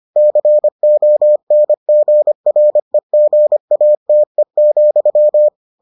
Morse code, heard as a keyed tone, about 25 wpm, 600 Hz. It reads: CONGREGATE,